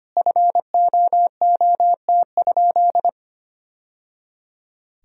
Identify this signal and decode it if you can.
Morse code, heard as a keyed tone, about 25 wpm, 700 Hz. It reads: FOOT?
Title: Sentences from Top 1000 Words - Review Part 3 - 25wpm